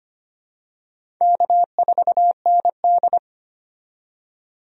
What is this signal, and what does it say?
Morse code, heard as a keyed tone, about 25 wpm, 700 Hz. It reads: K4ND
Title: USA Call Signs - 4 Characters - 25wpm